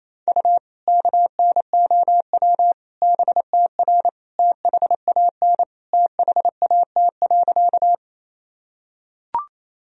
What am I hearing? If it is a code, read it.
Morse code, heard as a keyed tone, about 27 wpm, 700 Hz. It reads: U KNOW BTR THAN THAT.
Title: Sentences from Top 400 Words and Abbreviations 28wpm